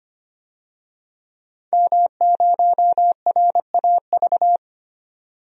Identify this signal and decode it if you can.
Morse code, heard as a keyed tone, about 25 wpm, 700 Hz. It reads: M0RAV
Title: DXCC-100 Call Signs - 5 Characters - 25wpm